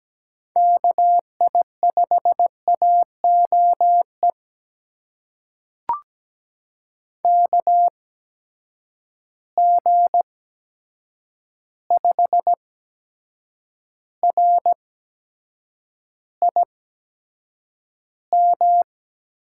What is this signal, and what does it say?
Morse code, heard as a keyed tone, about 17 wpm, spaced as for 7 wpm, 700 Hz. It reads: KI5AOE KG5RIM
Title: USA Call Signs - 6 Characters - 8x Character Spacing - 17wpm